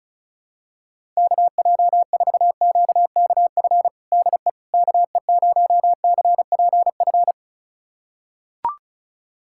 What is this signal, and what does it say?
Morse code, heard as a keyed tone, about 36 wpm, 700 Hz. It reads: KJ4QKF DE KE0CPF
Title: USA Callsign DE Callsign - 6 Characters - 35wpm